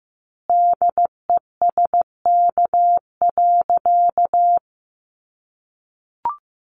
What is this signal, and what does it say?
Morse code, heard as a keyed tone, about 15 wpm, 700 Hz. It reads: DESK.
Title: A Sound Of Thunder 15wpm